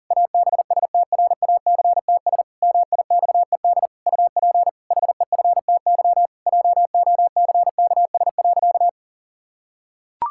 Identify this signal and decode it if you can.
Morse code, heard as a keyed tone, about 40 wpm, 700 Hz. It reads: ABSTRACTS MIXED UP HEFTY JOCKS.
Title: Pangram Challenge v2 - 40wpm